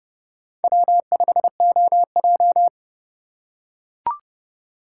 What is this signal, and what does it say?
Morse code, heard as a keyed tone, about 30 wpm, 700 Hz. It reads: W5OJ